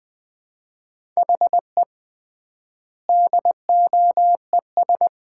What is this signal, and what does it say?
Morse code, heard as a keyed tone, about 20 wpm, 700 Hz. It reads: HE DOES
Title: 3x Word-Spacing - Sets of 5 Words - 20wpm